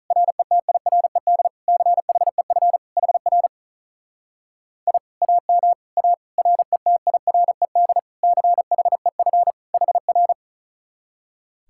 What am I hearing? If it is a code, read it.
Morse code, heard as a keyed tone, about 47 wpm, 700 Hz. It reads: RETIRED CHEF HR I AM A RETIRED CHEF HR